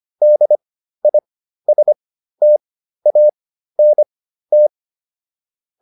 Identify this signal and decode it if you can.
Morse code, heard as a keyed tone, about 25 wpm, spaced as for 13 wpm, 600 Hz. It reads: DISTANT
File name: DIS7 13